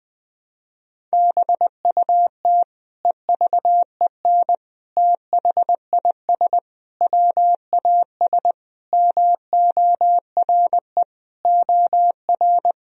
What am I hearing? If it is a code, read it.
Morse code, heard as a keyed tone, about 20 wpm, 700 Hz. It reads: BUT EVEN THIS WAS MORE OR
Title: Sentences from Top 1000 Words - Review Part 2 - 20wpm